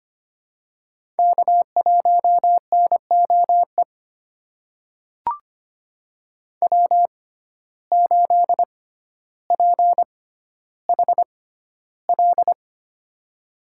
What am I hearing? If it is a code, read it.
Morse code, heard as a keyed tone, about 25 wpm, spaced as for 13 wpm, 700 Hz. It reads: K1NOE W8PHL
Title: USA Call Signs - 5 Characters - 6x Character Spacing - 25wpm